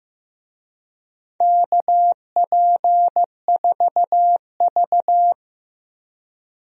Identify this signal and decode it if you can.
Morse code, heard as a keyed tone, about 15 wpm, 700 Hz. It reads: KP4V
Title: DXCC-100 Call Signs - 4 Characters - 15wpm